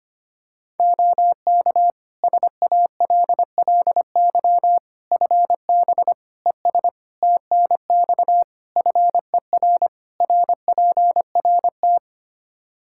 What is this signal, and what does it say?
Morse code, heard as a keyed tone, about 25 wpm, 700 Hz. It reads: OK SALLY FB ES TNX FER RPRT
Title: Practice QSO - Exchange 2 - Info - 25wpm